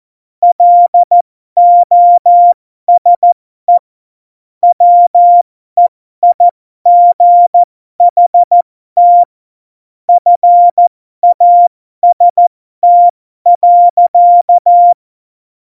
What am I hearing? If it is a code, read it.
Morse code, heard as a keyed tone, about 14 wpm, spaced as for 12 wpm, 700 Hz. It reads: LOSE WEIGHT FAST.
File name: Morsekurs / Witze